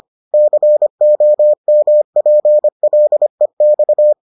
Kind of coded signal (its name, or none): Morse code